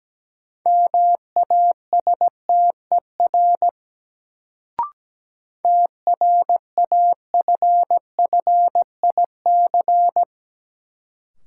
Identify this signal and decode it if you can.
Morse code, heard as a keyed tone, about 17 wpm, 700 Hz. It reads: MASTER TRAFFIC
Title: Sentences from Top 1300 Words - New Word Drill Alt - 17wpm